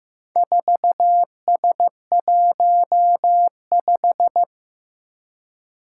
Morse code, heard as a keyed tone, about 15 wpm, 700 Hz. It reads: 4S15
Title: Four Challenging Letter-Number Combinations 15wpm